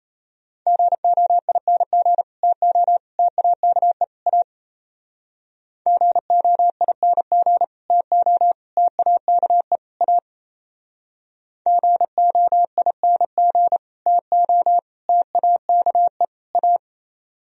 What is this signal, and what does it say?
Morse code, heard as a keyed tone, about 33 wpm, 700 Hz. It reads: GOING TO TAKE A GOING TO TAKE A GOING TO TAKE A
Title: Speed-Racing - Sets of 4 Words 28wpm